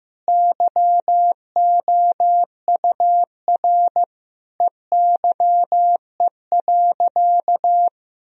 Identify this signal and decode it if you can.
Morse code, heard as a keyed tone, about 15 wpm, 700 Hz. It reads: YOUR EYE.
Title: Sentences from Top 1000 Words - Review Part 1 - 15wpm